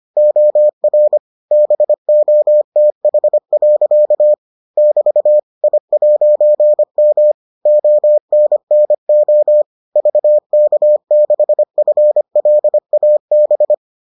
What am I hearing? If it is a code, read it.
Morse code, heard as a keyed tone, about 25 wpm, 600 Hz. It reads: OR BOTH. <BT> I'M ONNO VK6FLAB